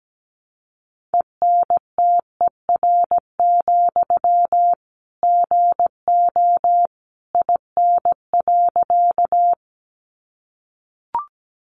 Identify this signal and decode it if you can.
Morse code, heard as a keyed tone, about 17 wpm, 700 Hz. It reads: ENTER, GO IN.
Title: Sentences from Top 1300 Words 17wpm